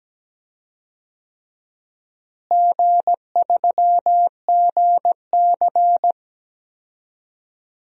Morse code, heard as a keyed tone, about 17 wpm, 700 Hz. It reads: G3GC
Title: DXCC-100 Call Signs - 4 Characters - Repeated 3x - 17wpm